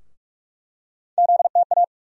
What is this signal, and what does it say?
Morse code, heard as a keyed tone, about 42 wpm, 700 Hz. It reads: ZTA
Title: USA Call Sign Postfixes - 3 Letters 45wpm